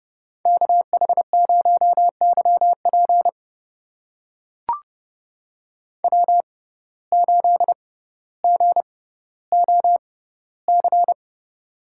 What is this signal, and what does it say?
Morse code, heard as a keyed tone, about 30 wpm, spaced as for 15 wpm, 700 Hz. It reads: KH0YP W8GOC
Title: USA Call Signs - 5 Characters - 6x Character Spacing - 30wpm